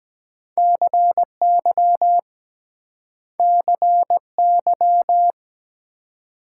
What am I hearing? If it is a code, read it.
Morse code, heard as a keyed tone, about 23 wpm, 700 Hz. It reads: CY CY